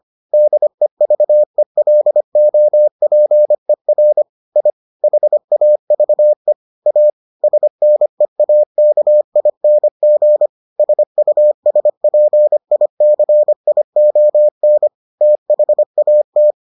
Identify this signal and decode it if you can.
Morse code, heard as a keyed tone, about 25 wpm, 600 Hz. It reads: DEVELOPER I HAVE A SNEAKING SUSPICION THAT